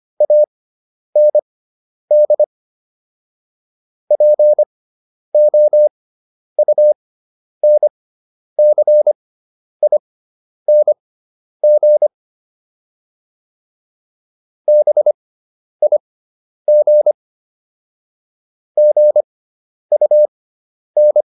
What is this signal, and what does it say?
Morse code, heard as a keyed tone, about 25 wpm, spaced as for 10 wpm, 600 Hz. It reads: AND POUNCING BIG GUN